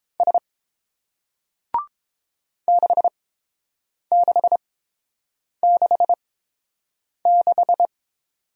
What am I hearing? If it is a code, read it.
Morse code, heard as a keyed tone, about 30 wpm, spaced as for 7 wpm, 700 Hz. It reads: S 6666